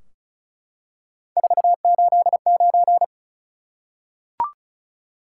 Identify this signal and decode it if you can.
Morse code, heard as a keyed tone, about 36 wpm, 700 Hz. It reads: 489
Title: Top 180 QSO Elements 35wpm